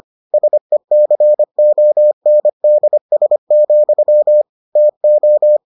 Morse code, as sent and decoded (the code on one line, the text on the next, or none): ... -.-. --- -. -.. ... --..-- / - ---
SECONDS, TO